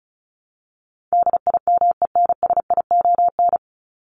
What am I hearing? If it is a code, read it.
Morse code, heard as a keyed tone, about 34 wpm, 700 Hz. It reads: DIMENSION
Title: Top 1501-1750 Words 35wpm